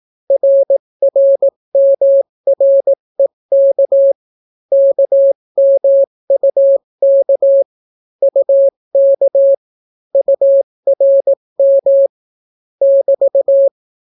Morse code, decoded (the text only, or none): RRMREK KMUK UK URM <BT>